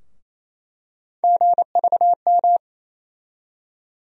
Morse code, heard as a keyed tone, about 28 wpm, 700 Hz. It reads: GVM
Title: USA Call Sign Postfixes - 3 Letters 28wpm